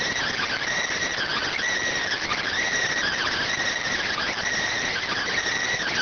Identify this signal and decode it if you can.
Emergency Alert System (SAME) tones, heard as harsh buzzing bursts. It is heard from 1.1 to 5.3 s.